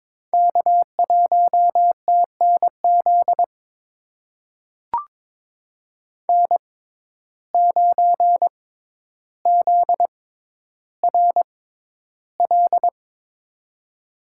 Morse code, heard as a keyed tone, about 22 wpm, spaced as for 11 wpm, 700 Hz. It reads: K1TNZ N9ZRL
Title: USA Call Signs - 5 Characters - 6x Character Spacing - 22wpm